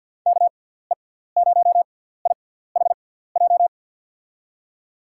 Morse code, heard as a keyed tone, about 52 wpm, spaced as for 17 wpm, 700 Hz. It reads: KE0IHJ
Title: USA Call Signs - 6 Characters - 6x Character Spacing - 50wpm